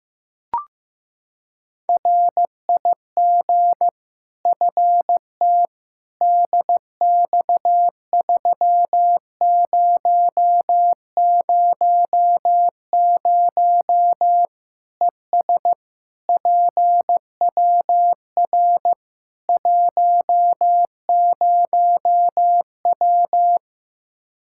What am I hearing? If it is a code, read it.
Morse code, heard as a keyed tone, about 15 wpm, 700 Hz. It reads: RIG FT DX3000 ES PWR 10W
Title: Practice QSO - Exchange 2 - Info - 15wpm